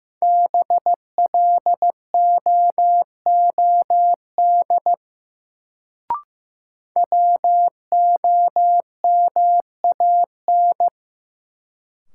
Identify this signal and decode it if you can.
Morse code, heard as a keyed tone, about 15 wpm, 700 Hz. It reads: BLOOD WOMAN
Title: Sentences from Top 1100 Words - New Word Drill Alt - 15wpm